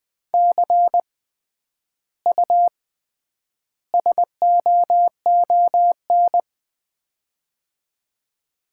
Morse code, heard as a keyed tone, about 20 wpm, 700 Hz. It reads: C U SOON
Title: Common QSO Phrases - 3x Word-Spacing - 20wpm